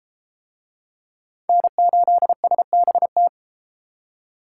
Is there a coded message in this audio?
Yes: Morse code from 1.5 to 3.3 s, heard as a keyed tone, about 32 wpm, 700 Hz. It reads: N8SBT